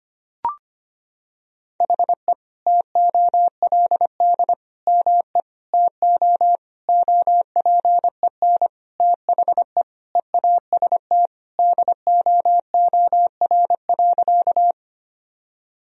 Morse code, heard as a keyed tone, about 25 wpm, 700 Hz. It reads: HE TOLD ME TO OPEN THE EAST DOOR.
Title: Sentences from Top 1000 Words - Review Part 3 - 25wpm